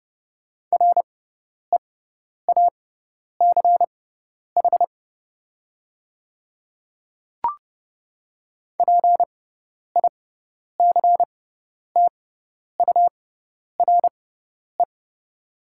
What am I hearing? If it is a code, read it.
Morse code, heard as a keyed tone, about 30 wpm, spaced as for 10 wpm, 700 Hz. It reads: REACH PICTURE
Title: 6x Character Spacing - Top 500 Words - 30wpm